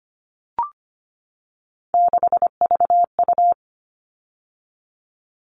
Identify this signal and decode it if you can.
Morse code, heard as a keyed tone, about 25 wpm, 700 Hz. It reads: 6VU
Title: All Cut Numbers - 3-Digit 25wpm